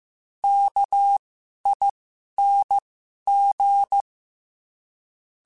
Morse code, heard as a keyed tone, about 15 wpm, spaced as for 11 wpm, 800 Hz. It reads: KING
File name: Special INGs-mp3s / Special INGs-mp3s / Sngl Ing @10